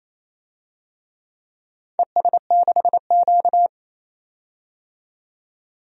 Morse code, heard as a keyed tone, about 27 wpm, 700 Hz. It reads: ES6Q
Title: DXCC-100 Call Signs - 4 Characters - 28wpm